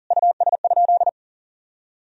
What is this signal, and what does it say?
Morse code, heard as a keyed tone, about 37 wpm, 700 Hz. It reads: US?
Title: Sentences from Top 1000 Words - Review Part 4 - 40wpm